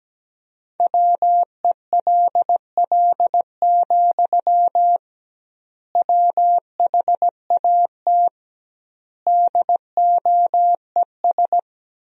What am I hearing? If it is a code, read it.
Morse code, heard as a keyed tone, about 17 wpm, 700 Hz. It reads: WELL, WHAT DOES
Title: 2x Word-Spacing - Sentences from Top 500 Words - 17wpm